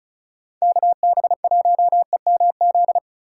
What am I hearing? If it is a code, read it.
Morse code, heard as a keyed tone, about 36 wpm, 700 Hz. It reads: KB1EMZ
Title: USA Call Signs - 6 Characters - 2x Character Spacing - 35wpm